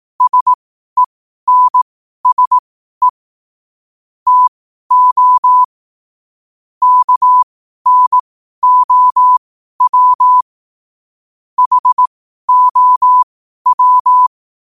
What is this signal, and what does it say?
Morse code, heard as a keyed tone, about 18 wpm, spaced as for 12 wpm, 1000 Hz. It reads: SENSE TO KNOW HOW
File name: qotdfiles / 12wpm